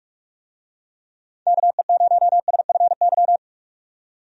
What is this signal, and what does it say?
Morse code, heard as a keyed tone, about 41 wpm, 700 Hz. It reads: KE0SFY
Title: USA Call Signs - 6 Characters - 45wpm